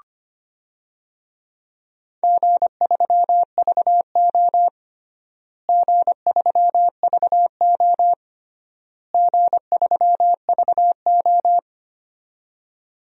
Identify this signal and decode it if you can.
Morse code, heard as a keyed tone, about 25 wpm, 700 Hz. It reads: G3VO G3VO G3VO
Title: DXCC-100 Call Signs - 4 Characters - Repeated 3x - 25wpm